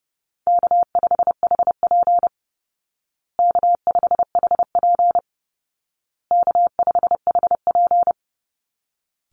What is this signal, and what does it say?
Morse code, heard as a keyed tone, about 30 wpm, 700 Hz. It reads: K5HP K5HP K5HP
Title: Call Signs Repeated 2x - 30wpm